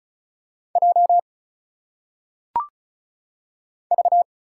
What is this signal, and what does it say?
Morse code, heard as a keyed tone, about 34 wpm, 700 Hz. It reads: J V
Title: Single Letters - Phonetic Pronunciation 35wpm